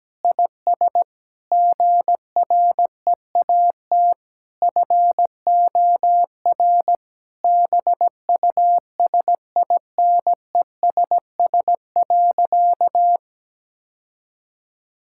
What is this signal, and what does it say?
Morse code, heard as a keyed tone, about 17 wpm, 700 Hz. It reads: IS GREAT FOR BUSINESS.
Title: Sentences from Top 1000 Words - Review Part 5 - 17wpm